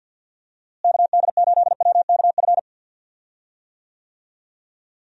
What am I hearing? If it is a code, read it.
Morse code, heard as a keyed tone, about 53 wpm, 700 Hz. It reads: KD8WKF
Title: USA Call Signs - 6 Characters - Repeated 3x - 50wpm